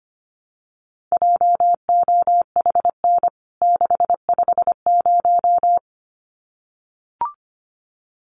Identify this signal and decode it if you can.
Morse code, heard as a keyed tone, about 25 wpm, 700 Hz. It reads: JOHN 650
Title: CWT Exchange 25wpm